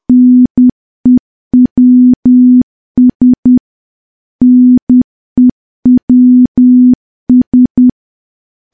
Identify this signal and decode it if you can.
Morse code, heard as a keyed tone, about 10 wpm, 250 Hz. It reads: NEWS NEWS
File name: morse / morse-training-10WPM